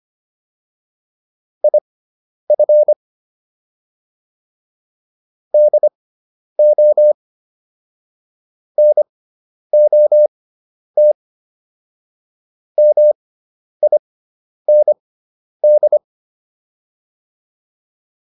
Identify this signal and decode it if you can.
Morse code, heard as a keyed tone, about 25 wpm, spaced as for 10 wpm, 600 Hz. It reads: IF DO NOT MIND
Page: intermediate-practice-files